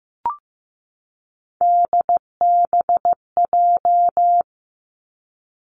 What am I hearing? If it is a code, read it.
Morse code, heard as a keyed tone, about 15 wpm, 700 Hz. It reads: DBJ
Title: Three Challenging Letter-Number Combinations 15wpm